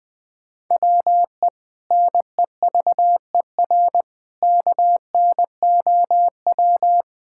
Morse code, heard as a keyed tone, about 20 wpm, 700 Hz. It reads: WE NEVER KNOW
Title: Sentences from Top 1600 Words 20wpm